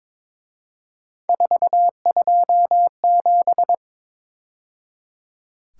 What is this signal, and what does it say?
Morse code, heard as a keyed tone, about 26 wpm, 700 Hz. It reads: 427
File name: Speed-Racing - Three Letter-Number Combinations - 22wpm